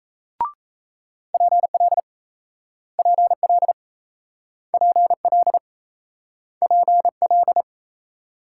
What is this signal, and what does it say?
Morse code, heard as a keyed tone, about 38 wpm, 700 Hz. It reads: PL PL PL PL